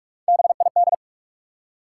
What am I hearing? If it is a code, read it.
Morse code, heard as a keyed tone, about 46 wpm, 700 Hz. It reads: BID
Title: Three Letter Words - 45wpm